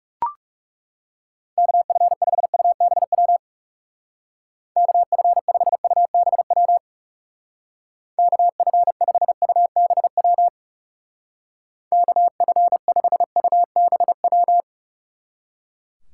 Morse code, heard as a keyed tone, about 38 wpm, 700 Hz. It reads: KF5UBW KF5UBW KF5UBW KF5UBW